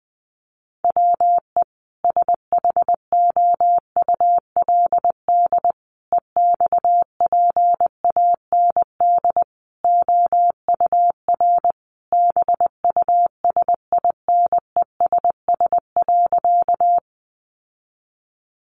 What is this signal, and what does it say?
Morse code, heard as a keyed tone, about 20 wpm, 700 Hz. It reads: WE SHOULD EXPAND OUR BUSINESS.
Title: Sentences from Top 1400 Words 20wpm